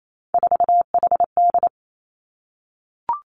Morse code, heard as a keyed tone, about 28 wpm, 700 Hz. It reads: 4HD
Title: Three Challenging Letter-Number Combinations 28wpm